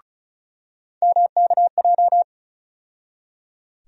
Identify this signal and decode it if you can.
Morse code, heard as a keyed tone, about 36 wpm, 700 Hz. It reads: MKJ
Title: USA Call Sign Postfixes - 3 Letters 35wpm